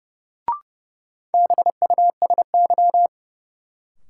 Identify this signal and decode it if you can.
Morse code, heard as a keyed tone, about 32 wpm, 700 Hz. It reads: BUSY
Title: Sentences from Top 1000 Words - Review Part 2 - New Word Drill - 30wpm